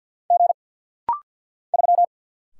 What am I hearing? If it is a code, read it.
Morse code, heard as a keyed tone, about 48 wpm, 700 Hz. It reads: G 3
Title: Single Letter-Number - Rapid-Fire 50wpm